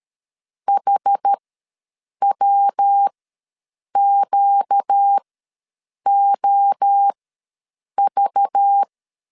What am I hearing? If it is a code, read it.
Morse code, heard as a keyed tone, about 13 wpm, spaced as for 7 wpm, 800 Hz. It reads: HWQOV
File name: LEZIONI / 5 LBVC / 5 LBV ws 12wpm 3015